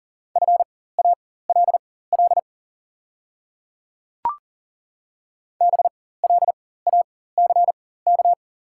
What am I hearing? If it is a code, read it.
Morse code, heard as a keyed tone, about 40 wpm, spaced as for 19 wpm, 700 Hz. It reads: FALL BLACK
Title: 4x Character Spacing - Top 500 Words - 40wpm